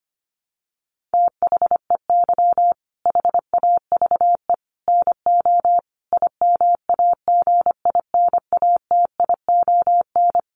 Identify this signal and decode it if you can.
Morse code, heard as a keyed tone, about 25 wpm, 700 Hz. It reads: THEY HAVE NO IMAGINATION
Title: Sentences from Top 1300 Words 25wpm